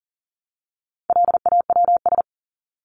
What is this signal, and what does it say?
Morse code, heard as a keyed tone, about 40 wpm, 700 Hz. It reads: LAWS
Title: ES:S Suffix - Top 100 Words with Base Words 40wpm